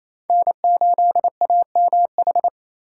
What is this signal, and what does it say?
Morse code, heard as a keyed tone, about 28 wpm, 700 Hz. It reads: N8AMH
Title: USA Call Signs - 5 Characters - 8x Character Spacing - 28wpm